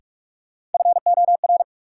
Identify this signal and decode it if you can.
Morse code, heard as a keyed tone, about 43 wpm, 700 Hz. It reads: FOR
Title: Sentences from Top 100 Words - New Word Drill - 45wpm